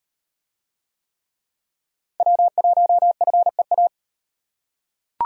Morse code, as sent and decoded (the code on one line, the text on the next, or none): .-- .---- ..-. .-
W1FEA